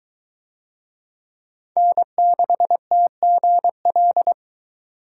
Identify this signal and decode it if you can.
Morse code, heard as a keyed tone, about 21 wpm, 700 Hz. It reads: N6TGL